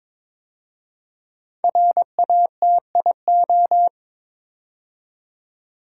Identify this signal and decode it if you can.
Morse code, heard as a keyed tone, about 22 wpm, 700 Hz. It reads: RATIO
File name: 5 Letter Words - First 3 Letters Given - 22wpm